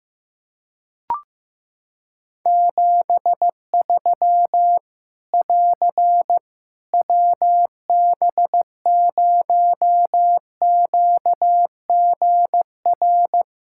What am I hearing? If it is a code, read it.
Morse code, heard as a keyed tone, about 15 wpm, 700 Hz. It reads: 73 <AR> WB0QGR
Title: Practice QSO - Exchange 4 - Ending - 15wpm